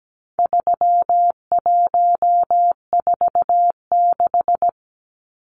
Morse code, heard as a keyed tone, about 17 wpm, 700 Hz. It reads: 3146
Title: Numbers - 4-Digit 17wpm